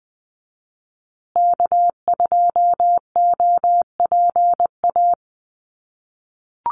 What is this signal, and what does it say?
Morse code, heard as a keyed tone, about 20 wpm, 700 Hz. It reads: K2OPA
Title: USA Call Signs Repeated 3x 20wpm